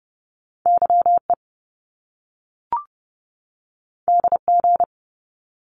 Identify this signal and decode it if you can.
Morse code, heard as a keyed tone, about 30 wpm, 700 Hz. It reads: YE DG
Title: Top Two Letter Combinations in Words 30wpm